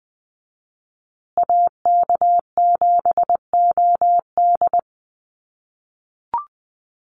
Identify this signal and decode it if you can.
Morse code, heard as a keyed tone, about 20 wpm, 700 Hz. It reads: AK7OD
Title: USA Call Signs Repeated 3x 20wpm